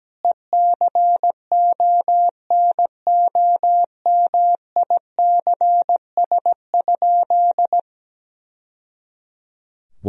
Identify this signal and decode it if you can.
Morse code, heard as a keyed tone, about 17 wpm, 700 Hz. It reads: ECONOMICS?
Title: Sentences from Top 1000 Words - Review Part 3 - 17wpm